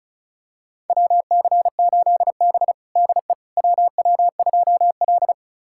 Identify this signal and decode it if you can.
Morse code, heard as a keyed tone, about 36 wpm, 700 Hz. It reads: WC8B DE WW2L
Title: USA Callsign DE Callsign - 4 Characters - 35wpm